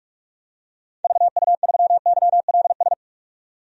Morse code, heard as a keyed tone, about 46 wpm, 700 Hz. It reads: VU3YLS
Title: DXCC-100 Call Signs - 6 Characters - 45wpm